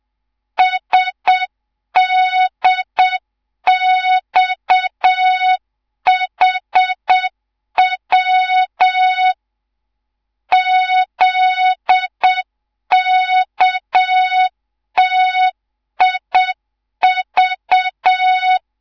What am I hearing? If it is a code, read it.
Morse code, heard as a keyed tone, about 7 wpm, 3700 Hz. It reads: SDXHW ZKTIV
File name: Morze-HA2MN / hang / 11 GYAKORLAT